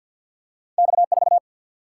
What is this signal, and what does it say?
Morse code, heard as a keyed tone, about 48 wpm, 700 Hz. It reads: X4
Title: Single-Letter Single-Number 50wpm